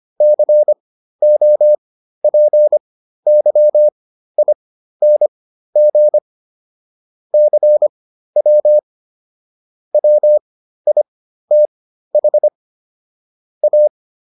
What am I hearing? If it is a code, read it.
Morse code, heard as a keyed tone, about 25 wpm, spaced as for 13 wpm, 600 Hz. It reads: COPYING CW WITH A